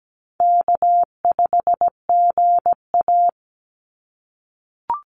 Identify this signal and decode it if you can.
Morse code, heard as a keyed tone, about 17 wpm, 700 Hz. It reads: K5GA
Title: Call Signs - 17wpm